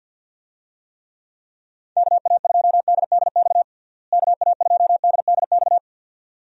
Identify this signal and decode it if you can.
Morse code, heard as a keyed tone, about 49 wpm, 700 Hz. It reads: KA2DDX KA2DDX
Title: USA Call Signs - 6 Characters - Repeated 2x - 50wpm